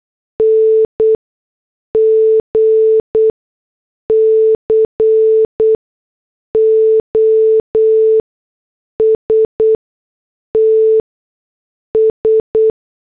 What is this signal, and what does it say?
Morse code, heard as a keyed tone, about 8 wpm, spaced as for 6 wpm, 440 Hz. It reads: NG COSTS
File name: morse / headlines